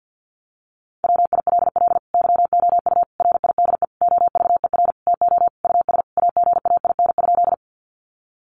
Morse code, heard as a keyed tone, about 50 wpm, 700 Hz. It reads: WILL YOU RIDE OVER TO US AGAIN?
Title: Sentences from Top 1200 Words 50wpm